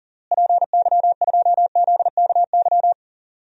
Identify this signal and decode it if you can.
Morse code, heard as a keyed tone, about 40 wpm, 700 Hz. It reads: PY2ZKY
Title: DXCC-100 Call Signs - 6 Characters - Repeated 3x - 40wpm